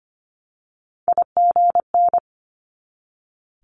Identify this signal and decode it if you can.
Morse code, heard as a keyed tone, about 25 wpm, 700 Hz. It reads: IGN
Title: Multi-letter Phonemes 25wpm